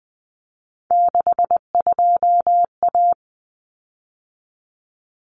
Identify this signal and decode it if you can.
Morse code, heard as a keyed tone, about 20 wpm, 700 Hz. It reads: 62A
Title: Common Cut Numbers - 3-Digit 20wpm